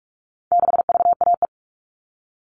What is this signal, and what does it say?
Morse code, heard as a keyed tone, about 44 wpm, 700 Hz. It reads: 6VAE
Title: All Cut Numbers - 4-Digit 45wpm